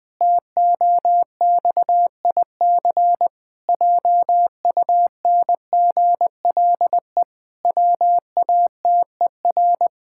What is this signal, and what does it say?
Morse code, heard as a keyed tone, about 20 wpm, 700 Hz. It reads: TOXIC JUNGLE WATER
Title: Pangram Challenge v2 - 20wpm